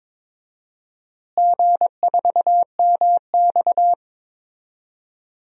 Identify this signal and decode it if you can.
Morse code, heard as a keyed tone, about 22 wpm, 700 Hz. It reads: G4MX
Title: DXCC-100 Call Signs - 4 Characters - Repeated 3x - 22wpm